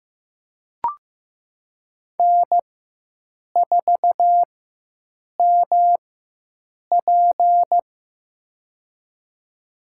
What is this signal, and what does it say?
Morse code, heard as a keyed tone, about 15 wpm, spaced as for 7 wpm, 700 Hz. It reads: N4MP
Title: USA Call Signs - 4x Character Spacing - 15wpm